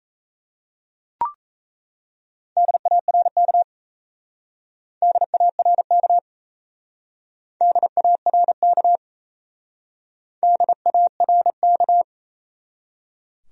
Morse code, heard as a keyed tone, about 33 wpm, 700 Hz. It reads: DARK DARK DARK DARK
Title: Speed-Racing Top 500 Words 28wpm